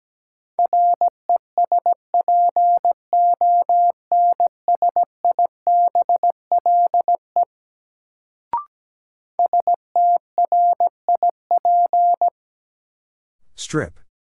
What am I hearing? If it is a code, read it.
Morse code, heard as a keyed tone, about 17 wpm, 700 Hz. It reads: RESPONSIBLE STRIP
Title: Sentences from Top 1600 Words - New Word Drill Alt - 17wpm